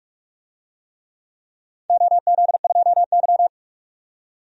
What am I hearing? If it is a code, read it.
Morse code, heard as a keyed tone, about 41 wpm, 700 Hz. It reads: OZ2Y
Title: DXCC-100 Call Signs - 4 Characters - 45wpm